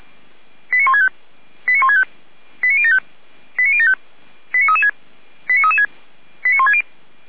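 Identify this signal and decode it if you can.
Five-tone selective call (ZVEI), heard as quick runs of five tones, tones of about 70 ms. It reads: 89165 89165 89075 89075 89336 89336 89160
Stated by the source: ZVEI five-tone call received by SDR, logged by Feuerwehr Orlamünde (fire brigade).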